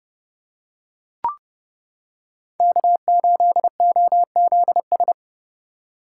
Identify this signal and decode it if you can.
Morse code, heard as a keyed tone, about 29 wpm, 700 Hz. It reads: K8OZS